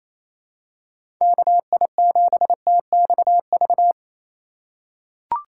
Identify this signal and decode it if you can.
Morse code, heard as a keyed tone, about 28 wpm, 700 Hz. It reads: KI7TXV
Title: USA Call Signs - 6 Characters - 2x Character Spacing - 28wpm